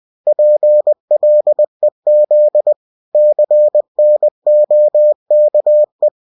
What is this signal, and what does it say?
Morse code, heard as a keyed tone, about 20 wpm, 600 Hz. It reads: PLEZ CNOKE
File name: cw-books / winnie-the-pooh / winnie-20wpm-page-0029